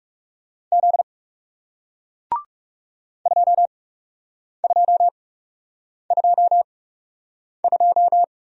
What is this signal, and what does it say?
Morse code, heard as a keyed tone, about 41 wpm, spaced as for 8 wpm, 700 Hz. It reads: Z 2222